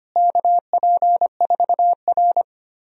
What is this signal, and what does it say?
Morse code, heard as a keyed tone, about 25 wpm, 700 Hz. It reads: KP4R
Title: DXCC-100 Call Signs - 4 Characters - Repeated 3x - 25wpm